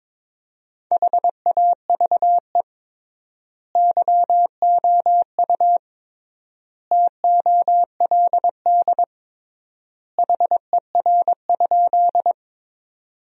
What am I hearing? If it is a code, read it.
Morse code, heard as a keyed tone, about 22 wpm, 700 Hz. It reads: HAVE YOU TOLD HER?
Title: 3x Word-Spacing - Sentences from Top 400 Words - 22wpm